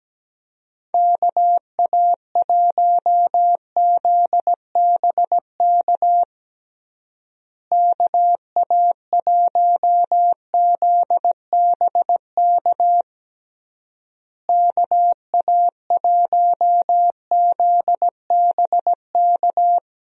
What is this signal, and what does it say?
Morse code, heard as a keyed tone, about 17 wpm, 700 Hz. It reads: KA1ZBK KA1ZBK KA1ZBK
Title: USA Call Signs - 6 Characters - Repeated 3x - 17wpm